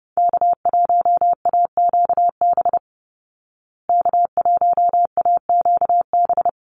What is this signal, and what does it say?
Morse code, heard as a keyed tone, about 30 wpm, 700 Hz. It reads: K1AQB K1AQB
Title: Call Signs Repeated 3x - 30wpm